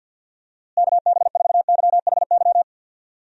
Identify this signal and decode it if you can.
Morse code, heard as a keyed tone, about 49 wpm, 700 Hz. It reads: KB4YHY
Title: USA Call Signs - 6 Characters - Repeated 3x - 50wpm